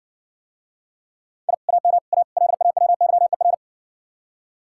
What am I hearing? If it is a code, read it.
Morse code, heard as a keyed tone, about 60 wpm, 700 Hz. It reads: I AM A LAWYER